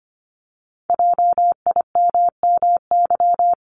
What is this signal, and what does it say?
Morse code, heard as a keyed tone, about 25 wpm, 700 Hz. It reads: JIMMY
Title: Multi-letter Phonemes 25wpm